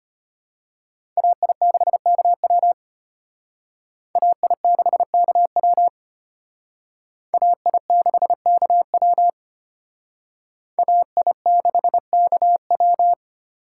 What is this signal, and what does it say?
Morse code, heard as a keyed tone, about 32 wpm, 700 Hz. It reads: AI6KW AI6KW AI6KW AI6KW